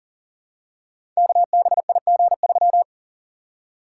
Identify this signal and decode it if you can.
Morse code, heard as a keyed tone, about 42 wpm, 700 Hz. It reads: KBIG3
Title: Code Groups - 5 Letter:Numbers - 40wpm